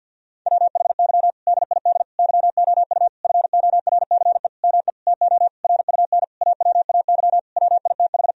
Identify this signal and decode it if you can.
Morse code, heard as a keyed tone, about 50 wpm, 700 Hz. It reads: WHY DID YOU FORCE ME TO RUN AWAY WITH
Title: Sentences from Top 1000 Words - Review Part 3 - 50wpm